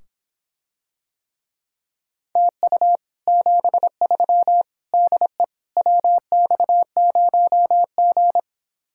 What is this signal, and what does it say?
Morse code, heard as a keyed tone, about 26 wpm, 700 Hz. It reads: TU 73 DE WX0G